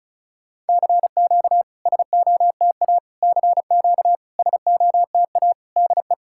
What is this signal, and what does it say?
Morse code, heard as a keyed tone, about 36 wpm, 690 Hz. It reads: CQ SOTA CQ SOTA DE
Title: SOTA QSO Practice 35wpm